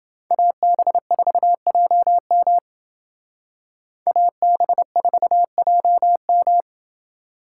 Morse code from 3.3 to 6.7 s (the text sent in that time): AB4JM